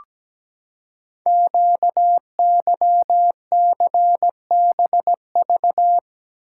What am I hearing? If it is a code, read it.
Morse code, heard as a keyed tone, about 17 wpm, 700 Hz. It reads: QYCBV
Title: Code Groups - 5 Letters - 17wpm